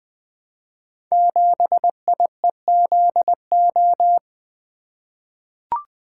Morse code, heard as a keyed tone, about 20 wpm, 700 Hz. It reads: 7IEZO
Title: Code Groups - 5 Letter:Numbers - 20wpm